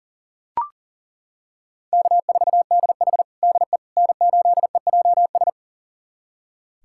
Morse code, heard as a keyed tone, about 40 wpm, 700 Hz. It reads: K4DH DE N8EJS
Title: USA Callsign DE Callsign 40wpm